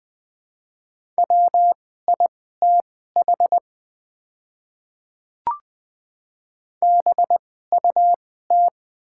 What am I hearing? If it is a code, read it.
Morse code, heard as a keyed tone, about 20 wpm, spaced as for 15 wpm, 690 Hz. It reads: WITH BUT
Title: 2x Character Spacing - Top 200 Words - 20wpm